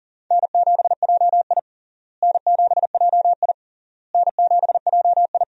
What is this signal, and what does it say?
Morse code, heard as a keyed tone, about 40 wpm, 700 Hz. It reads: N7JI N7JI N7JI
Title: USA Call Signs - 4 Characters - Repeated 3x - 40wpm